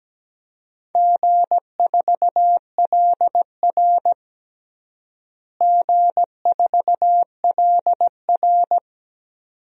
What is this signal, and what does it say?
Morse code, heard as a keyed tone, about 17 wpm, 700 Hz. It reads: G4LR G4LR
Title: DXCC-100 Call Signs - 4 Characters - Repeated 2x - 17wpm